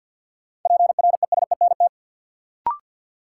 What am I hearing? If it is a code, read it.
Morse code, heard as a keyed tone, about 52 wpm, 700 Hz. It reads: PRESENT
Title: Sentences from Top 800 Words - New Word Drill - 50wpm